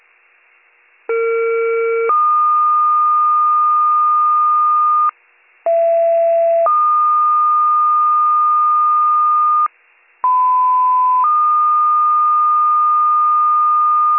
Two-tone page, heard as a one-second tone, then a three-second tone, about 460 then 1220 Hz, about 670 then 1220 Hz, about 1010 then 1220 Hz. It is heard from 1.1 to 14.2 s.